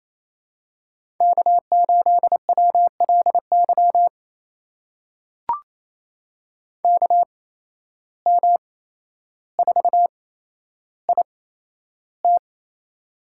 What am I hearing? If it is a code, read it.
Morse code, heard as a keyed tone, about 27 wpm, spaced as for 12 wpm, 700 Hz. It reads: K8WLY KM4IT
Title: USA Call Signs - 5 Characters - 8x Character Spacing - 28wpm